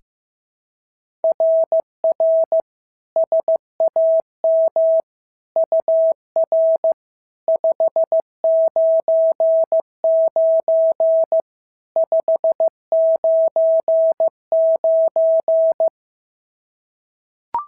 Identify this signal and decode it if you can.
Morse code, heard as a keyed tone, about 14 wpm, 650 Hz. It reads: RR SAM UR 599 599
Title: SOTA QSO Practice 15wpm